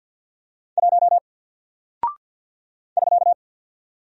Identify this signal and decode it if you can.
Morse code, heard as a keyed tone, about 50 wpm, 700 Hz. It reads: 1 <SK>
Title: Single Character (Letter:Number:Punctuation:Prosign) Alt 50wpm